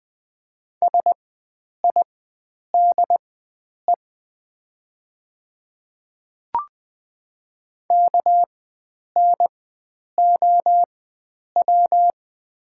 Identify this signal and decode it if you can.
Morse code, heard as a keyed tone, about 20 wpm, spaced as for 9 wpm, 700 Hz. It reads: SIDE KNOW
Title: 4x Character Spacing - Top 100 Words - 20wpm